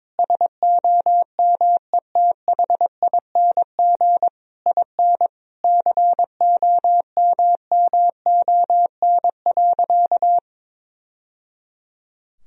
Morse code, heard as a keyed tone, about 22 wpm, 700 Hz. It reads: SOMETHING IN COMMON.
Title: Sentences from Top 1000 Words - Review Part 3 - 22wpm